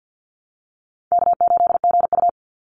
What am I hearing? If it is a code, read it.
Morse code, heard as a keyed tone, about 50 wpm, 700 Hz. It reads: K8GU
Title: Call Signs - 50wpm